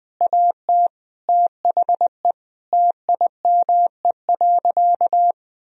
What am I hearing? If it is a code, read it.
Morse code, heard as a keyed tone, about 20 wpm, 700 Hz. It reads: AT THE TIME.
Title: Sentences from Top 1000 Words - Review Part 2 - 20wpm